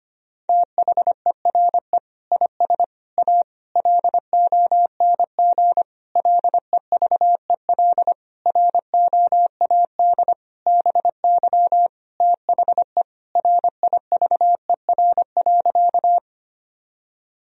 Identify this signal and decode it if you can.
Morse code, heard as a keyed tone, about 25 wpm, 700 Hz. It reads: THERE IS A LONG LEVEL ROAD BY THE RIVER.
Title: Sentences from Top 1000 Words - Review Part 4 - 25wpm